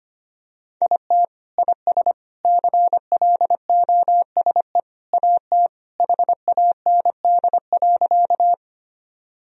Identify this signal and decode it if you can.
Morse code, heard as a keyed tone, about 25 wpm, 700 Hz. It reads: IT IS CLOSE AT HAND.
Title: Sentences from Top 1000 Words - Review Part 1 - 25wpm